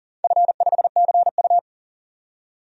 Morse code, heard as a keyed tone, about 40 wpm, 700 Hz. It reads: F5CU
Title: DXCC-100 Call Signs - 4 Characters - 40wpm